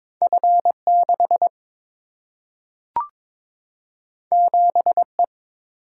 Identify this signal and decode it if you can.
Morse code, heard as a keyed tone, about 22 wpm, 700 Hz. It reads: F 6 7 E
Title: Single-Letter Single-Number 22wpm